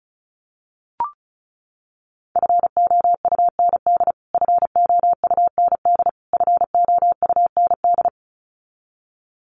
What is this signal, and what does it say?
Morse code, heard as a keyed tone, about 34 wpm, 700 Hz. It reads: FOUND FOUND FOUND
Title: Top 300 Words Repeated 35wpm